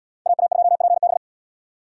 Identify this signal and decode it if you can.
Morse code, heard as a keyed tone, about 77 wpm, 690 Hz. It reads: AA2PB